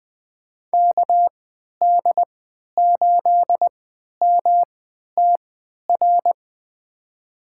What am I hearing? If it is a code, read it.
Morse code, heard as a keyed tone, about 20 wpm, spaced as for 11 wpm, 700 Hz. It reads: KD8MTR
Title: USA Call Signs - 6 Characters - 3x Character Spacing - 20wpm